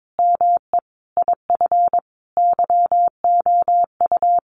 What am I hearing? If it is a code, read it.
Morse code, heard as a keyed tone, about 22 wpm, 700 Hz. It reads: ME IF YOU
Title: Sentences from Top 1000 Words 22wpm